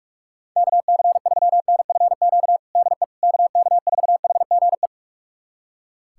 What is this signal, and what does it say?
Morse code, heard as a keyed tone, about 41 wpm, 700 Hz. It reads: KC3NFQ DE KK4HGE
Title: USA Callsign DE Callsign - 6 Characters - 45wpm